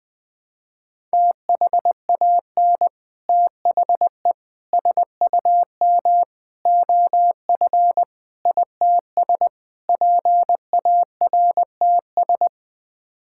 Morse code, heard as a keyed tone, about 20 wpm, 700 Hz. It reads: THAN THE SUM OF ITS PARTS
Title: Sets of 6 Words - 20wpm